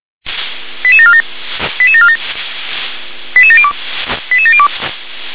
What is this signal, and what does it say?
Five-tone selective call (ZVEI), heard as quick runs of five tones, tones of about 70 ms. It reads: 88747 88747 88702 88702
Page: SDR/ZVEI